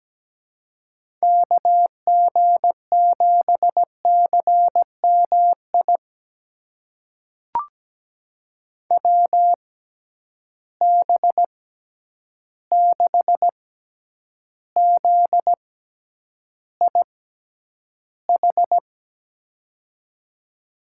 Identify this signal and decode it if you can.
Morse code, heard as a keyed tone, about 17 wpm, spaced as for 9 wpm, 700 Hz. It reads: KG7CMI WB6ZIH
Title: USA Call Signs - 6 Characters - 6x Character Spacing - 17wpm